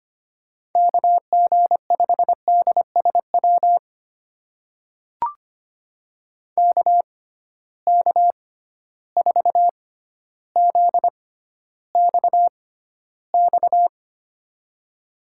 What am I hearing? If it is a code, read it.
Morse code, heard as a keyed tone, about 25 wpm, spaced as for 13 wpm, 700 Hz. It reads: KG5DSW KK4ZXX
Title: USA Call Signs - 6 Characters - 6x Character Spacing - 25wpm